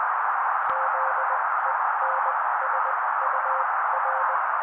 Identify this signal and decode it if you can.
Morse code, heard as a keyed tone, about 20 wpm, spaced as for 16 wpm, 530 Hz, with noise. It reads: ZENSUR
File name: daily / 16wpm noise